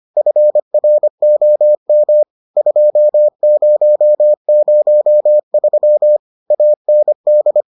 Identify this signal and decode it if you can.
Morse code, heard as a keyed tone, about 25 wpm, 600 Hz. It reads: FROM 2003 AND